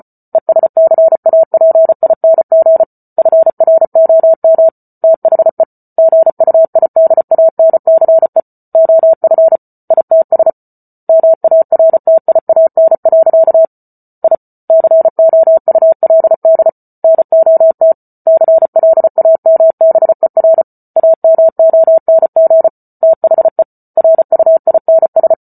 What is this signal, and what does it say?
Morse code, heard as a keyed tone, about 34 wpm, 650 Hz. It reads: ESCAPING FROM THE GUIDANCE OF ITS MARTIAN. I COULD NOT CLAMBER AMONG THE RUINS